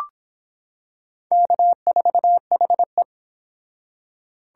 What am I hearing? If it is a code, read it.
Morse code, heard as a keyed tone, about 25 wpm, 700 Hz. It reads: K4HE